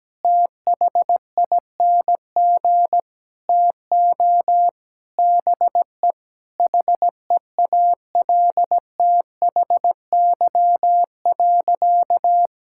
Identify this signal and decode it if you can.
Morse code, heard as a keyed tone, about 17 wpm, 700 Hz. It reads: THING TO BE HEALTHY.
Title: Sentences from Top 1000 Words - Review Part 5 - 17wpm